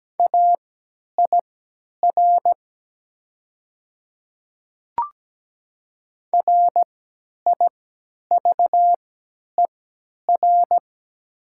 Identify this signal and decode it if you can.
Morse code, heard as a keyed tone, about 17 wpm, spaced as for 10 wpm, 700 Hz. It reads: AIR RIVER
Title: 3x Character Spacing - Top 300 Words - 17wpm